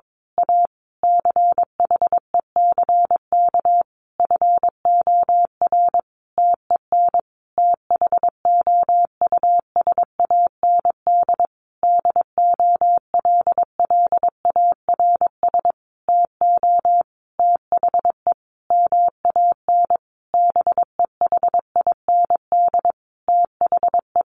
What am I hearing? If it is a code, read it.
Morse code, heard as a keyed tone, about 22 wpm, 700 Hz. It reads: A CHECK FOR TEN THOUSAND DOLLARS TO THE MAN BEHIND THE
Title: A Sound Of Thunder 22wpm